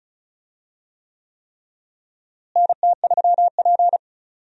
Morse code, heard as a keyed tone, about 36 wpm, 700 Hz. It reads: NT3P
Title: USA Call Signs - 4 Characters - Repeated 3x - 35wpm